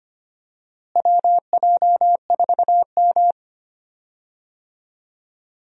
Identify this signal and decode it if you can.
Morse code, heard as a keyed tone, about 25 wpm, 700 Hz. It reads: WJ4M
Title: USA Call Signs - 4 Characters - Repeated 3x - 25wpm